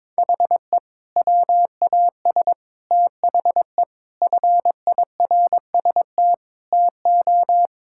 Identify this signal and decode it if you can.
Morse code, heard as a keyed tone, about 21 wpm, 700 Hz. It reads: HE WAS THE FIRST TO
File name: practice / 22wpm / Sentences with 2-Word Phrases - 22wpm